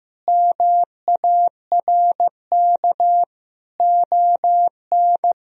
Morse code, heard as a keyed tone, about 15 wpm, 700 Hz. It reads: MARK ON
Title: Sentences from Top 1000 Words - Review Part 1 - 15wpm